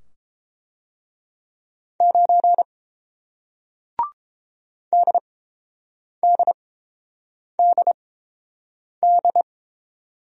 Morse code, heard as a keyed tone, about 30 wpm, spaced as for 7 wpm, 700 Hz. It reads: 9 DDDD